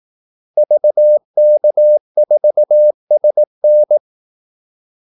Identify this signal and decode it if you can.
Morse code, heard as a keyed tone, about 18 wpm, 600 Hz. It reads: VK4SN
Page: Kaldesignaler 18 wpm